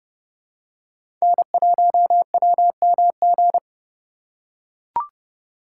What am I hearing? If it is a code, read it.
Morse code, heard as a keyed tone, about 31 wpm, 700 Hz. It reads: N1WMG
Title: USA Call Signs - 5 Characters - 4x Character Spacing - 30wpm